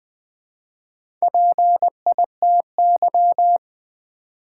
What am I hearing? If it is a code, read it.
Morse code, heard as a keyed tone, about 20 wpm, 700 Hz. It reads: PITY
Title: Build A Word - 2 Letters At A Time - 4-Letter Words - 20wpm